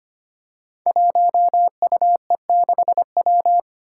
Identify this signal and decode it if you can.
Morse code, heard as a keyed tone, about 25 wpm, 700 Hz. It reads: 1UE6W
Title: Five Challenging Letter-Number Combinations 25wpm